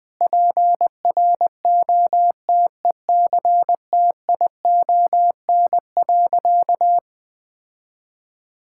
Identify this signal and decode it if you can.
Morse code, heard as a keyed tone, about 21 wpm, 700 Hz. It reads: PROTECTION.
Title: Sentences from Top 1600 Words 20wpm